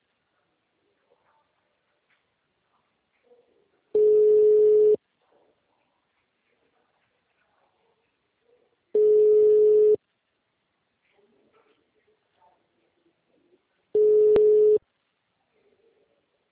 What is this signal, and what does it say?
Ringback tone (Europe, 425 Hz).